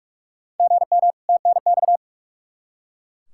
Morse code, heard as a keyed tone, about 43 wpm, 700 Hz. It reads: GM TNX
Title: Practice QSO - Exchange 1 - Beginning - 45wpm